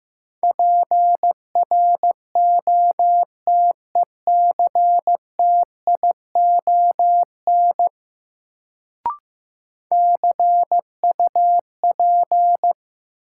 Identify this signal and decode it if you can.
Morse code, heard as a keyed tone, about 15 wpm, 700 Hz. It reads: PROTECTION CUP
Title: Sentences from Top 1600 Words - New Word Drill Alt - 15wpm